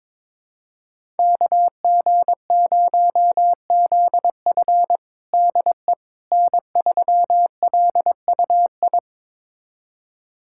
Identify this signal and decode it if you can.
Morse code, heard as a keyed tone, about 22 wpm, 700 Hz. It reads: KG0ZF DE N3LUI
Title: USA Callsign DE Callsign - 5 Characters - 22wpm